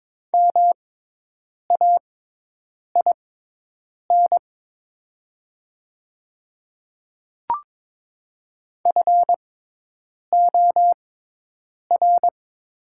Morse code, heard as a keyed tone, about 22 wpm, spaced as for 8 wpm, 700 Hz. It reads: MAIN FOR
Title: 6x Character Spacing - Top 300 Words - 22wpm